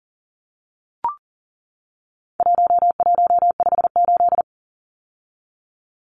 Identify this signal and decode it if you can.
Morse code, heard as a keyed tone, about 39 wpm, 700 Hz. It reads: 1158
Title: Numbers - 4-Digit 40wpm